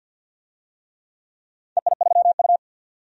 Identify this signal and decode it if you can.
Morse code, heard as a keyed tone, about 50 wpm, 700 Hz. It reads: EI3U
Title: DXCC-100 Call Signs - 4 Characters - 50wpm